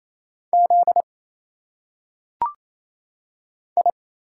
Morse code, heard as a keyed tone, about 28 wpm, 700 Hz. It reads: Z I
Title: Single Letters - Phonetic Pronunciation 28wpm